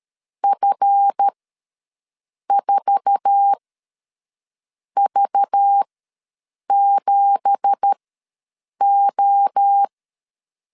Morse code, heard as a keyed tone, about 13 wpm, spaced as for 6 wpm, 800 Hz. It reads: F4V7O
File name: LEZIONI / 10 SIGNS / 10B INT ws 12wpm 3527